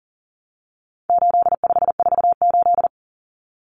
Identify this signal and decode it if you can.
Morse code, heard as a keyed tone, about 40 wpm, 700 Hz. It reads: 8548
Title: Numbers - 4-Digit 40wpm